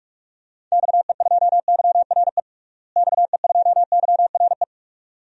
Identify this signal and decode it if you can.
Morse code, heard as a keyed tone, about 41 wpm, 700 Hz. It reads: XE2YRE XE2YRE
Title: DXCC-100 Call Signs - 6 Characters - Repeated 2x - 45wpm